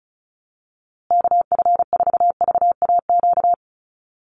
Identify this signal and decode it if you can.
Morse code, heard as a keyed tone, about 34 wpm, 700 Hz. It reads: KF4VAQ
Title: USA Call Signs Repeated 3x 35wpm